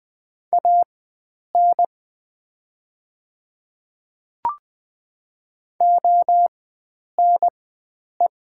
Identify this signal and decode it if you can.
Morse code, heard as a keyed tone, about 20 wpm, spaced as for 9 wpm, 700 Hz. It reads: AN ONE
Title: 4x Character Spacing - Top 100 Words - 20wpm